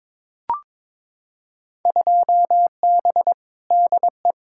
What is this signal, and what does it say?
Morse code, heard as a keyed tone, about 22 wpm, 700 Hz. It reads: 2B DE
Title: Field Day Exchanges Encore - 22wpm